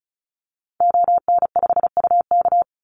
Morse code, heard as a keyed tone, about 34 wpm, 700 Hz. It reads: ON5UK
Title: Call Signs - 35wpm